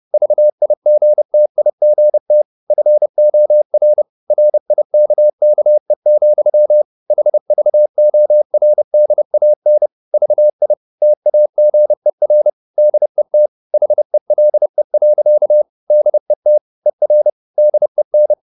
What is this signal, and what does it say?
Morse code, heard as a keyed tone, about 30 wpm, 600 Hz. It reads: VIGTIGT FOR RIKKE, HVORDAN VI TAGER DET HELE. DET ER DEN
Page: Ferie 30 wpm